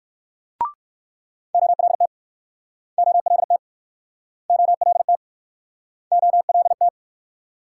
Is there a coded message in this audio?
Yes: Morse code from 1.5 to 6.9 s, heard as a keyed tone, about 59 wpm, 700 Hz. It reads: OLT OLT OLT OLT